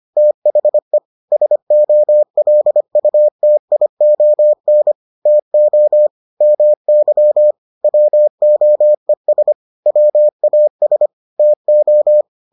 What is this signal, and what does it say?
Morse code, heard as a keyed tone, about 25 wpm, 600 Hz. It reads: THE SOLUTION TO MY WOES WAS TO